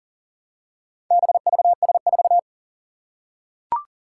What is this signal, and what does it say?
Morse code, heard as a keyed tone, about 40 wpm, 700 Hz. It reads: BVS4
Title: Four Challenging Letter-Number Combinations 40wpm